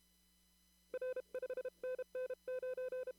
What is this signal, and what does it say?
Morse code, heard as a keyed tone, about 31 wpm, 510 Hz. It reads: R 5NN 9
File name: CW / kt5j wpx cw-op-k5pi